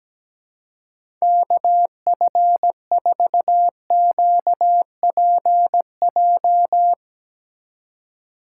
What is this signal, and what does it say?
Morse code, heard as a keyed tone, about 17 wpm, 700 Hz. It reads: KF4QPJ
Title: USA Call Signs - 6 Characters - 4x Character Spacing - 17wpm